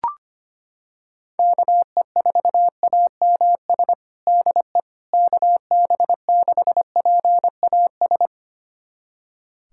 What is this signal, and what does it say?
Morse code, heard as a keyed tone, about 25 wpm, 700 Hz. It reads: KE4AMS DE KB6PAS
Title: USA Callsign DE Callsign 25wpm